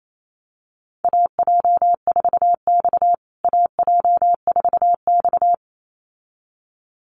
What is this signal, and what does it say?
Morse code, heard as a keyed tone, about 28 wpm, 700 Hz. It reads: AJ4X AJ4X
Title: USA Call Signs Repeated 2x 28wpm